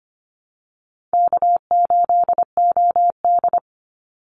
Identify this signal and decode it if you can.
Morse code, heard as a keyed tone, about 25 wpm, 700 Hz. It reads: K8OD
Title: Easy Call Signs Repeated 2x - 25wpm